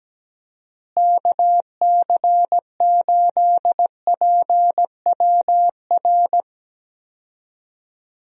Morse code, heard as a keyed tone, about 17 wpm, 700 Hz. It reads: KC8PWR
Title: USA Call Signs - 6 Characters - Repeated 3x - 17wpm